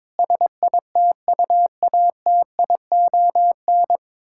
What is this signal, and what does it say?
Morse code, heard as a keyed tone, about 22 wpm, 700 Hz. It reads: SITUATION
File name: Sets of 9 Words - 22wpm